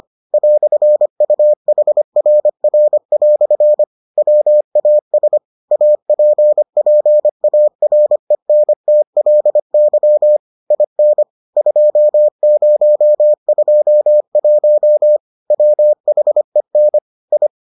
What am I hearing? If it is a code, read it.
Morse code, heard as a keyed tone, about 25 wpm, 600 Hz. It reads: "UHRR" WAS APPARENTLY IN 2021 WHEN I